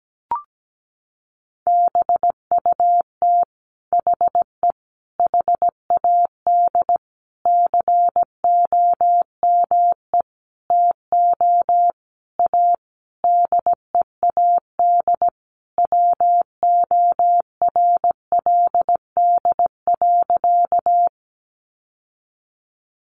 Morse code, heard as a keyed tone, about 17 wpm, 700 Hz. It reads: BUT HE HAD COME TO A DEAD WORLD.
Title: Sentences from Top 1500 Words 17wpm